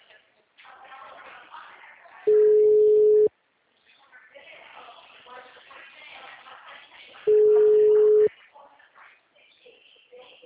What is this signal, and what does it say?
Ringback tone (Europe, 425 Hz), with speech.